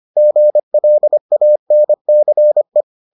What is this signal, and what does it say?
Morse code, heard as a keyed tone, about 25 wpm, 600 Hz. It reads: GLANCE